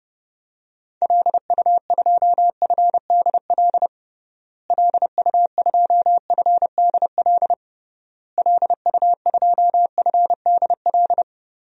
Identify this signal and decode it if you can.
Morse code, heard as a keyed tone, about 30 wpm, 700 Hz. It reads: LU2FDL LU2FDL LU2FDL
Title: DXCC-100 Call Signs - 6 Characters - Repeated 3x - 30wpm